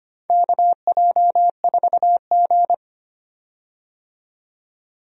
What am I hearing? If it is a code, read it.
Morse code, heard as a keyed tone, about 25 wpm, 700 Hz. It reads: KJ4G
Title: USA Call Signs - 4 Characters - 25wpm